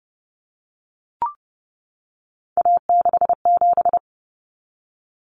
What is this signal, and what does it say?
Morse code, heard as a keyed tone, about 30 wpm, 700 Hz. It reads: A67
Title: Common Cut Numbers - 3-Digit 30wpm